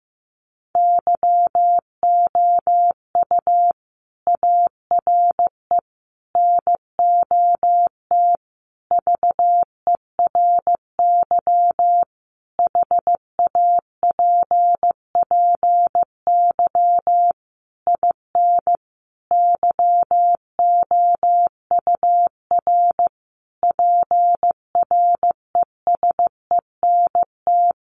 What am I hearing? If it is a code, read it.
Morse code, heard as a keyed tone, about 15 wpm, 700 Hz. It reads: YOU ARE NOT VERY HAPPY IN YOUR PRESENT
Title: Sentences from Top 1200 Words 15wpm